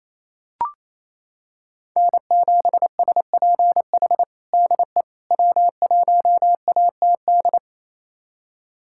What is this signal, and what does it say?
Morse code, heard as a keyed tone, about 27 wpm, 700 Hz. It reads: N7SPH DE W1ATD
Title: USA Callsign DE Callsign - 5 Characters - 28wpm